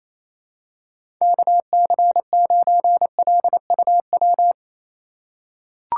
Morse code, heard as a keyed tone, about 28 wpm, 700 Hz. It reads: KC9LUW
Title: USA Call Signs - 6 Characters - 2x Character Spacing - 28wpm